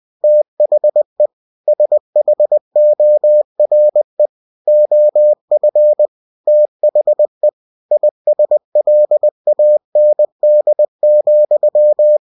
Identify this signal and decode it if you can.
Morse code, heard as a keyed tone, about 20 wpm, 600 Hz. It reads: THE SHORE OF THE ISLAND,
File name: cw-books / winnie-the-pooh / winnie-20wpm-page-0099